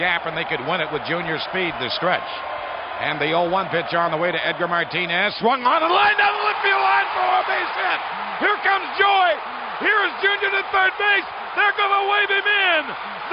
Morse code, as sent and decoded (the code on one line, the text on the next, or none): none